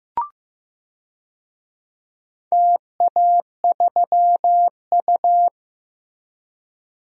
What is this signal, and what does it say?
Morse code, heard as a keyed tone, about 15 wpm, 700 Hz. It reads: TA3U
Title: DXCC-100 Call Signs - 4 Characters - Repeated 2x - 15wpm